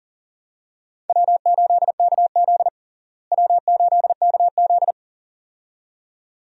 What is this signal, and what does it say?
Morse code, heard as a keyed tone, about 37 wpm, 700 Hz. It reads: W8KZ W8KZ
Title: USA Call Signs - 4 Characters - Repeated 2x - 40wpm